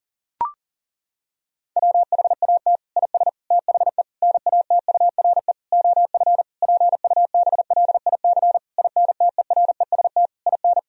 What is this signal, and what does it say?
Morse code, heard as a keyed tone, about 40 wpm, 700 Hz. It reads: WHAT IS THE NATURE OF PUBLIC INTEREST IN
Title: Sentences from Top 1000 Words - Review Part 3 - 40wpm